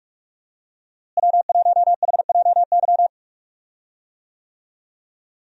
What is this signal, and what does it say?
Morse code, heard as a keyed tone, about 41 wpm, 700 Hz. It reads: W1HJY
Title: USA Call Signs - 5 Characters - Repeated 2x - 45wpm